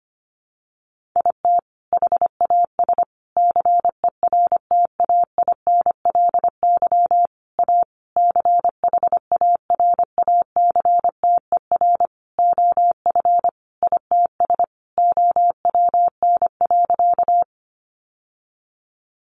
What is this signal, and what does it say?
Morse code, heard as a keyed tone, about 25 wpm, 700 Hz. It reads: IT HAS CERTAINLY A CHARACTER OF ITS OWN.
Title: Sentences from Top 1200 Words 25wpm